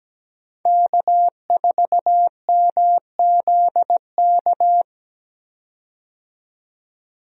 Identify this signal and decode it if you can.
Morse code, heard as a keyed tone, about 17 wpm, 700 Hz. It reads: K4MZK
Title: USA Call Signs - 5 Characters - 17wpm